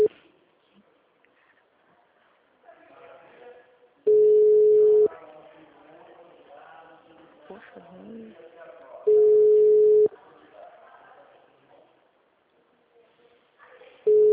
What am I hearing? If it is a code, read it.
Ringback tone (Europe, 425 Hz), with speech.
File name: out- -2022- - - .12584